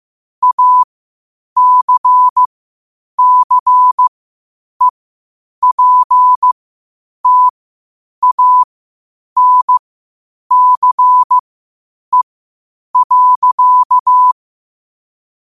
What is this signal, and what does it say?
Morse code, heard as a keyed tone, about 15 wpm, spaced as for 8 wpm, 1000 Hz. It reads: ACCEPTANCE.
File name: qotdfiles / 8wpm